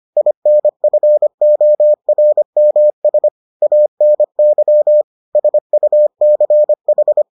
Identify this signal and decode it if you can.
Morse code, heard as a keyed tone, about 25 wpm, 600 Hz. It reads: INFORMS ANY SUCH